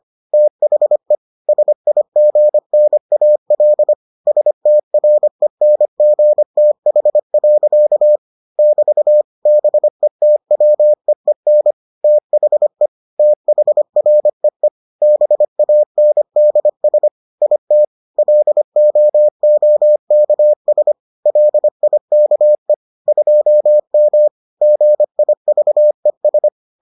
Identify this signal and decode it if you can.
Morse code, heard as a keyed tone, about 25 wpm, 600 Hz. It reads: THE SIGNAL STRENGTH. <BT> BETWEEN THE THREE BANDS IT LOOKS LIKE 2M GIVES